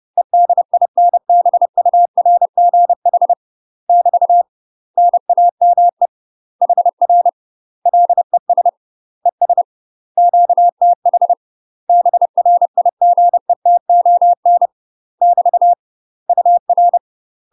Morse code, heard as a keyed tone, about 30 wpm, 700 Hz. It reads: EDINBURGH <BT> NAME HR LES ES QTH BRIGETON <BT> UR
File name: UK MASTER QSOs TRAINING / 30WPM / UK MASTER QSOs 0028